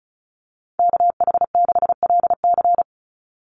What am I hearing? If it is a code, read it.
Morse code, heard as a keyed tone, about 34 wpm, 700 Hz. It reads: KH6LC
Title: Call Signs Repeated 3x - 35wpm